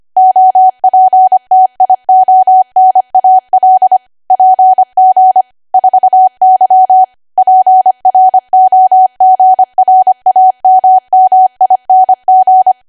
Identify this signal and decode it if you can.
Morse code, heard as a keyed tone, about 25 wpm, 750 Hz. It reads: OPTIONAL PG 4Y PROGRAMMING
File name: Archive / 25 WPM / 25WPM